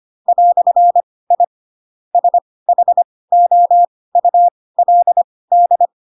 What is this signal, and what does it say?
Morse code, heard as a keyed tone, about 25 wpm, spaced as for 18 wpm, 700 Hz. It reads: "I SHOULD
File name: Pinocchio ch3 12 25 ik1hge